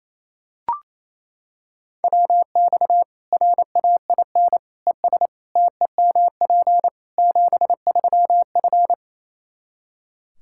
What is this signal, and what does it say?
Morse code, heard as a keyed tone, about 28 wpm, 700 Hz. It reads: WX RAIN ES TEMP 73F
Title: Practice QSO - Exchange 2 - Info - 28wpm